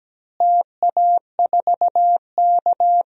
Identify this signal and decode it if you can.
Morse code, heard as a keyed tone, about 17 wpm, 700 Hz. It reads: TA4K
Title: DXCC-100 Call Signs - 4 Characters - Repeated 2x - 17wpm